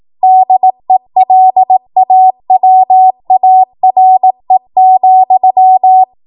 Morse code, heard as a keyed tone, about 18 wpm, 750 Hz. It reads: DELAWARE,
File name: Morse / 18WPM